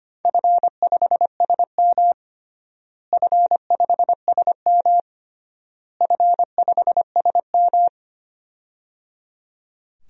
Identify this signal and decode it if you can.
Morse code, heard as a keyed tone, about 25 wpm, 700 Hz. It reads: F5SM F5SM F5SM
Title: DXCC-100 Call Signs - 4 Characters - Repeated 3x - 25wpm